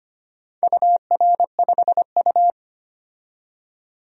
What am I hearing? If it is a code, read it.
Morse code, heard as a keyed tone, about 25 wpm, 700 Hz. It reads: UR5U
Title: DXCC-100 Call Signs - 4 Characters - 25wpm